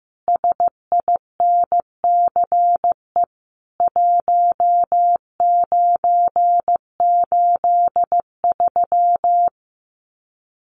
Morse code, heard as a keyed tone, about 15 wpm, 700 Hz. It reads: SINCE 1983
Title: Common QSO Phrases 15wpm